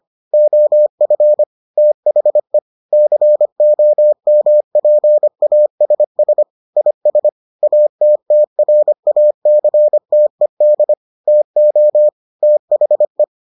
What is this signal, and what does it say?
Morse code, heard as a keyed tone, about 25 wpm, 600 Hz. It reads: OF THE COMPASS IS ATTRACTED TO THE